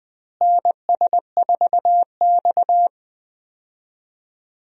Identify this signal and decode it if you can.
Morse code, heard as a keyed tone, about 20 wpm, 700 Hz. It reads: NS4X
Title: USA Call Signs - 4 Characters - Repeated 3x - 20wpm